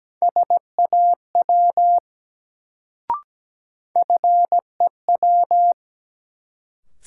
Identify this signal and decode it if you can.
Morse code, heard as a keyed tone, about 17 wpm, 700 Hz. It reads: SAW FEW
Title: Sentences from Top 300 Words - New Word Drill - 17wpm